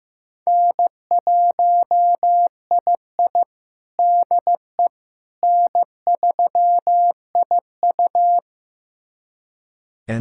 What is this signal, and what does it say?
Morse code, heard as a keyed tone, about 15 wpm, 700 Hz. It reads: N1II DE N3IU
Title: USA Callsign DE Callsign - 4 Characters - 15wpm